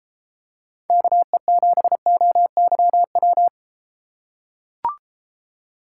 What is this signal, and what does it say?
Morse code, heard as a keyed tone, about 31 wpm, 700 Hz. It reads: KE7OYW